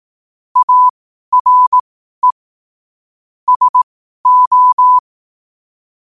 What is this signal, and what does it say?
Morse code, heard as a keyed tone, about 18 wpm, spaced as for 12 wpm, 1000 Hz. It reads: ARE SO
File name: qotdfiles / 12wpm